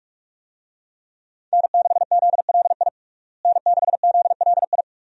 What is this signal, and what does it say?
Morse code, heard as a keyed tone, about 46 wpm, 700 Hz. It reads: N6ZLI N6ZLI
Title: USA Call Signs - 5 Characters - Repeated 2x - 45wpm